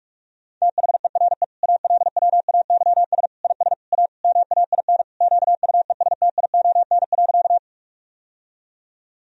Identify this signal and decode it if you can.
Morse code, heard as a keyed tone, about 41 wpm, 700 Hz. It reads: THERE ALWAYS IS A MAIN QUESTION.
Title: Sentences from Top 1000 Words - Review Part 2 - 45wpm